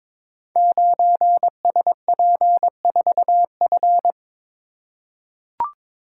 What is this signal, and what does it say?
Morse code, heard as a keyed tone, about 22 wpm, 700 Hz. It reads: 9SP4F
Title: Code Groups - 5 Letter:Numbers - 22wpm